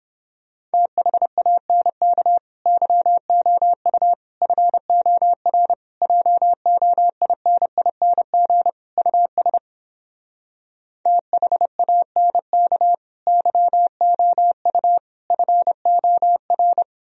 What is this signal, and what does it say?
Morse code, heard as a keyed tone, about 29 wpm, 700 Hz. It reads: THANK YOU FOR JOINING US THANK YOU FOR